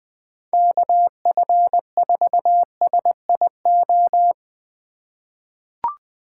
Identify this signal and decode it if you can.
Morse code, heard as a keyed tone, about 20 wpm, 700 Hz. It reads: KF4SIO
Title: USA Call Signs - 6 Characters - 2x Character Spacing - 20wpm